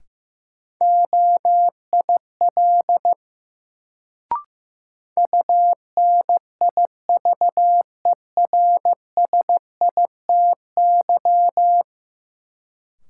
Morse code, heard as a keyed tone, about 15 wpm, 700 Hz. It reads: OIL UNIVERSITY
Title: Sentences from Top 1000 Words - Review Part 4 - New Word Drill - 15wpm